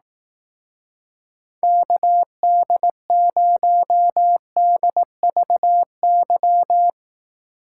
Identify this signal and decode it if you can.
Morse code, heard as a keyed tone, about 18 wpm, 700 Hz. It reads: KD0DVY
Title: Speed Racing - USA 2x3 Call Signs 15wpm